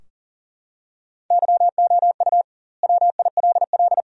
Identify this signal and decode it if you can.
Morse code, heard as a keyed tone, about 40 wpm, 700 Hz. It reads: YOU WILL
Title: 2x Word-Spacing - Sentences from Top 400 Words - 40wpm